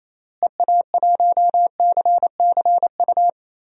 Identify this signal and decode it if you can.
Morse code, heard as a keyed tone, about 28 wpm, 700 Hz. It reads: EA1CCU
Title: DXCC-100 Call Signs - 6 Characters - Repeated 3x - 28wpm